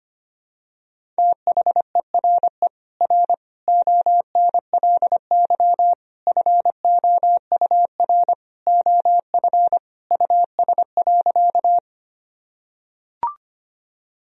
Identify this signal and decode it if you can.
Morse code, heard as a keyed tone, about 25 wpm, 700 Hz. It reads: THERE R ONLY FOUR OF US.
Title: Sentences from Top 300 Words and Abbreviations 25wpm